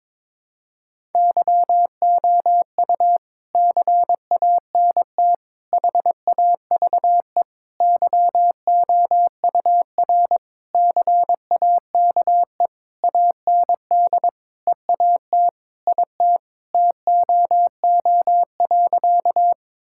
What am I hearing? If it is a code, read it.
Morse code, heard as a keyed tone, about 22 wpm, 700 Hz. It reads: YOU CANT HAVE YOUR CAKE AND EAT IT TOO.
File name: Sentences with 9-Word Phrases - 22wpm